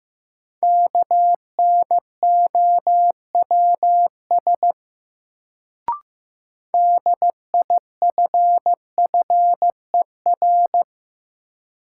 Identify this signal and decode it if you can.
Morse code, heard as a keyed tone, about 16 wpm, 700 Hz. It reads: KNOWS DIFFER
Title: Sentences from Top 1000 Words - Review Part 1 - New Word Drill - 15wpm